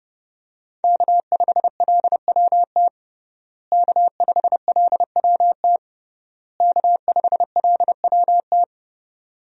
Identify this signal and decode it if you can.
Morse code, heard as a keyed tone, about 32 wpm, 700 Hz. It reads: K5LWT K5LWT K5LWT
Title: USA Call Signs - 5 Characters - Repeated 3x - 30wpm